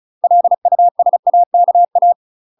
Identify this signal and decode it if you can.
Morse code, heard as a keyed tone, about 35 wpm, 700 Hz. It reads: LUSAKA
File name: UK MASTER QSOs TRAINING/35WPM